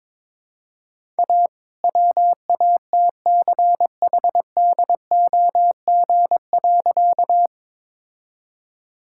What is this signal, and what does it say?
Morse code, heard as a keyed tone, about 22 wpm, 700 Hz. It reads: A WATCHDOG.
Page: morse-code-ninja-practice-22wpm